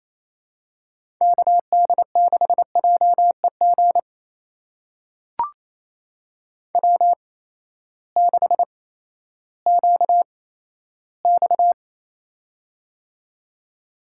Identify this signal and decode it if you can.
Morse code, heard as a keyed tone, about 28 wpm, spaced as for 14 wpm, 700 Hz. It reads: KD6JEG W6QX
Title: USA Call Signs - 8x Character Spacing - 28wpm